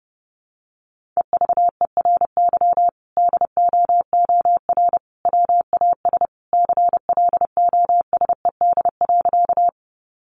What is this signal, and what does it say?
Morse code, heard as a keyed tone, about 30 wpm, 700 Hz. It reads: EVERY DOOR WAS CLOSED.
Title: Sentences from Top 1400 Words 30wpm